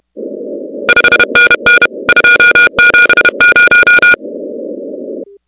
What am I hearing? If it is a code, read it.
Morse code, heard as a keyed tone, about 31 wpm, 1440 Hz, with noise. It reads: 5NN 270
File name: contests / 2019-cw-wpx-kt5j-k5pi- -2330 0-16